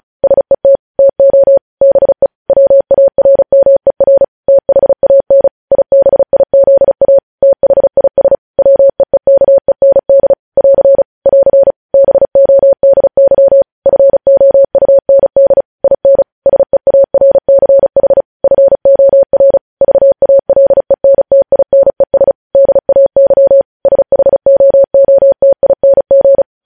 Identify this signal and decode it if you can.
Morse code, heard as a keyed tone, about 36 wpm, 550 Hz. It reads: SET TO BE WARMER THAN IBIZA THIS WEEKEND <AR> <AR> BODY FOUND IN SEARCH FOR VALENTINES DAY SHOOTING
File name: podcast / News headlines-1x-35x35